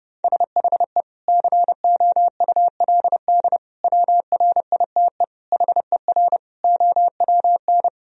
Morse code, heard as a keyed tone, about 32 wpm, 700 Hz. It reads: SHE COULD WRITE HER OWN
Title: Sentences from Top 1000 Words - Review Part 1 - 30wpm